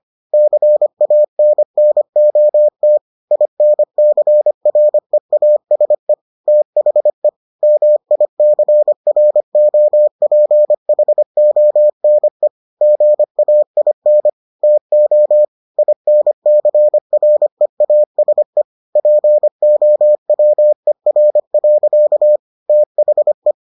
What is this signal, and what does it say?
Morse code, heard as a keyed tone, about 25 wpm, 600 Hz. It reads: CANNOT INCREASE THE MICROPHONE GAIN TO INCREASE POWER. THE